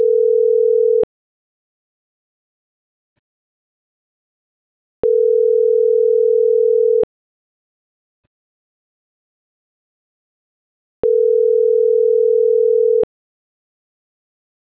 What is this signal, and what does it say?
Ringback tone (North America), 440 + 480 Hz.